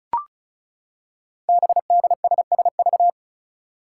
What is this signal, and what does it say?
Morse code, heard as a keyed tone, about 36 wpm, 700 Hz. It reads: BDSSV
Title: Five Challenging Letter-Number Combinations 35wpm